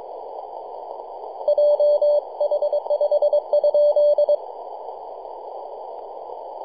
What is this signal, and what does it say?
Morse code, heard as a keyed tone, about 23 wpm, 590 Hz. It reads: JH5 ?
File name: 3d2ya-18-085-cw